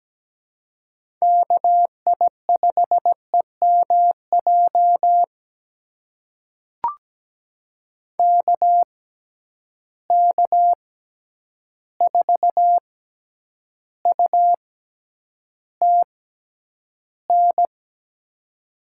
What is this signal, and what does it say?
Morse code, heard as a keyed tone, about 17 wpm, spaced as for 9 wpm, 700 Hz. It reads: KI5EMJ KK4UTN
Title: USA Call Signs - 6 Characters - 6x Character Spacing - 17wpm